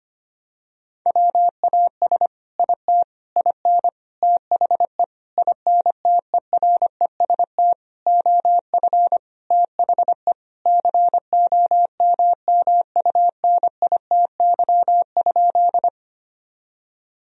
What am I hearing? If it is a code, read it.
Morse code, heard as a keyed tone, about 25 wpm, 700 Hz. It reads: WAS IT IN THE INTEREST OF THE COMMUNITY?
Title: Sentences from Top 1000 Words - Review Part 4 - 25wpm